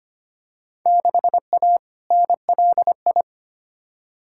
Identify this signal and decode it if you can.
Morse code, heard as a keyed tone, about 25 wpm, 700 Hz. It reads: 6A NLI
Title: Field Day Exchanges Encore - 25wpm